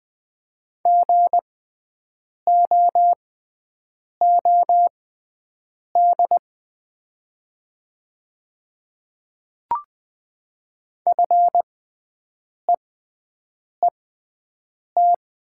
Morse code, heard as a keyed tone, about 20 wpm, spaced as for 7 wpm, 700 Hz. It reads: GOOD FEET
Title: 6x Character Spacing - Top 500 Words - 20wpm